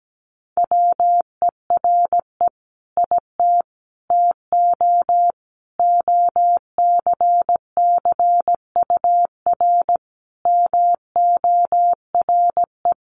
Morse code, heard as a keyed tone, about 17 wpm, 700 Hz. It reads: WERE IT TO OCCUR MORE
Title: Sentences from Top 1300 Words 17wpm